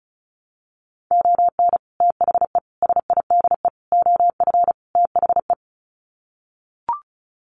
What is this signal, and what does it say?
Morse code, heard as a keyed tone, about 34 wpm, 700 Hz. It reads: ON THE SIDE OF THE
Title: Sets of 5 Words 35wpm